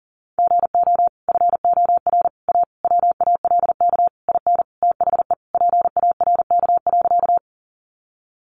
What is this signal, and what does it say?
Morse code, heard as a keyed tone, about 40 wpm, 700 Hz. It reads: GO FOR A WALK IN THE PARK.
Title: Sentences from Top 1300 Words 40wpm